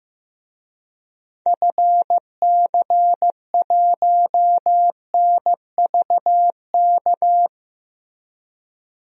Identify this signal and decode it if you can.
Morse code, heard as a keyed tone, about 15 wpm, 700 Hz. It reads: FC1NVK
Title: DXCC-100 Call Signs - 6 Characters - Repeated 3x - 15wpm